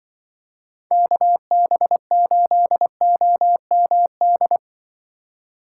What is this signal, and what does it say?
Morse code, heard as a keyed tone, about 25 wpm, 700 Hz. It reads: KB8OMD
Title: Speed Racing - USA 2x3 Call Signs 20wpm